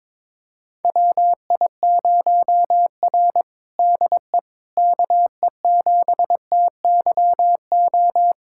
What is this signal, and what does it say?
Morse code, heard as a keyed tone, about 22 wpm, 700 Hz. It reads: WI0R DE KE7TYO
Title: USA Callsign DE Callsign 22wpm